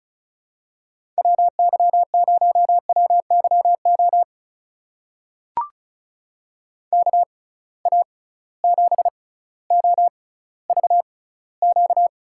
Morse code, heard as a keyed tone, about 36 wpm, spaced as for 18 wpm, 700 Hz. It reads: WY0WYO KA7OVQ
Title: USA Call Signs - 6 Characters - 6x Character Spacing - 35wpm